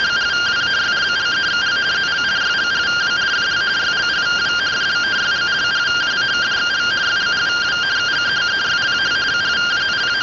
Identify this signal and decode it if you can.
Two-tone teleprinter signal, about 170 Hz shift.
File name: findings / axial iodine 3mm 205.gen-15 rtty rx